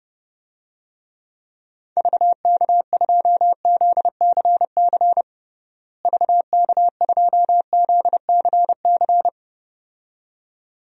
Morse code, heard as a keyed tone, about 31 wpm, 700 Hz. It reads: VK2ZCC VK2ZCC
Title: DXCC-100 Call Signs - 6 Characters - Repeated 2x - 30wpm